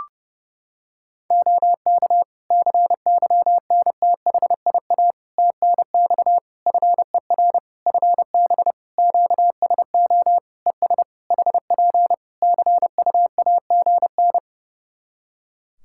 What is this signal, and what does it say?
Morse code, heard as a keyed tone, about 30 wpm, 700 Hz. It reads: OK CYNTHIA TNX FER FB QSO ES HP CUAGN
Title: K1USN SST QSO Practice 30wpm